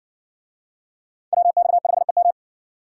Morse code, heard as a keyed tone, about 55 wpm, 700 Hz. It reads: WX5EM